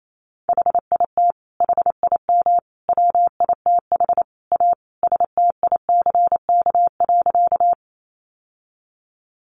Morse code, heard as a keyed tone, about 28 wpm, 700 Hz. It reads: HIT HIM WITH A STICK.
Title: Sentences from Top 1200 Words 28wpm